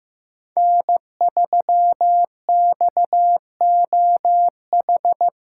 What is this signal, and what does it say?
Morse code, heard as a keyed tone, about 18 wpm, 700 Hz. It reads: N3XOH
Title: Speed Racing - USA 1x3 Call Signs 15wpm